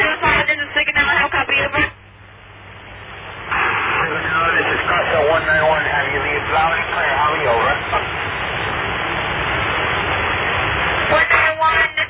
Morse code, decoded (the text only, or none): none